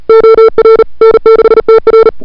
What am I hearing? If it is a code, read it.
Morse code, heard as a keyed tone, about 34 wpm, 440 Hz. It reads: OR N6TR
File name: CWSprint / February 2007 CW Sprint / 01-2007 CW Sprint Hour1